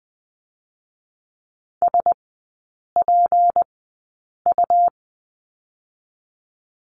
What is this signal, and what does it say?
Morse code, heard as a keyed tone, about 20 wpm, spaced as for 8 wpm, 700 Hz. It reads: SPU
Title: 152 Lesson 25e - Letter p - Three Characters 20wpm